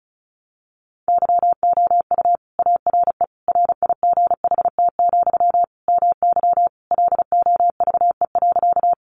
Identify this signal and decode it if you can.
Morse code, heard as a keyed tone, about 34 wpm, 700 Hz. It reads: YOU ARE RIGHT, MY LOVE.
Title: Sentences from Top 400 Words 35wpm